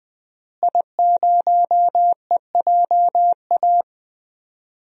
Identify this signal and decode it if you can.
Morse code, heard as a keyed tone, about 20 wpm, 700 Hz. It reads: I0EJA
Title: DXCC-100 Call Signs - 5 Characters - Repeated 3x - 20wpm